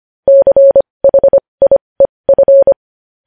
Morse code, heard as a keyed tone, about 25 wpm, spaced as for 20 wpm, 550 Hz. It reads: CHIEF
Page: news-headlines-in-morse-code-at-20-wpm